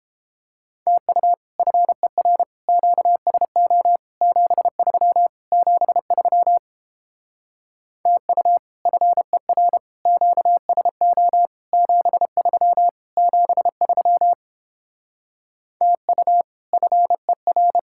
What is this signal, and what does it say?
Morse code, heard as a keyed tone, about 31 wpm, 700 Hz. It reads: TU FER QSO 73 73 TU FER QSO 73 73 TU FER